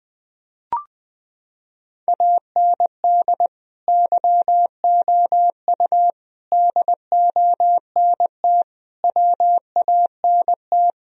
Morse code, heard as a keyed tone, about 20 wpm, 700 Hz. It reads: AND YOU DONT WANT
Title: Sentences from Top 1000 Words - Review Part 2 - 20wpm